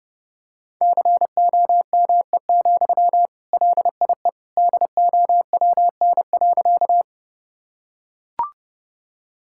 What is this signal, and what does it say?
Morse code, heard as a keyed tone, about 32 wpm, 700 Hz. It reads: COME, LIE DOWN.
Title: Sentences from Top 1600 Words 30wpm